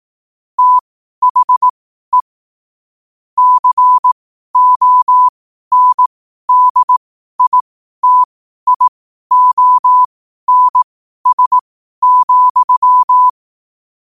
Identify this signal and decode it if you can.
Morse code, heard as a keyed tone, about 18 wpm, spaced as for 12 wpm, 1000 Hz. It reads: THE CONDITIONS,
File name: qotdfiles / 12wpm